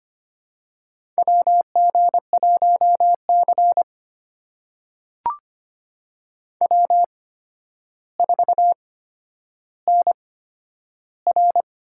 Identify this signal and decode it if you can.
Morse code, heard as a keyed tone, about 25 wpm, spaced as for 11 wpm, 700 Hz. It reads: WG1C W4NR
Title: USA Call Signs - 4 Characters - 8x Character Spacing - 25wpm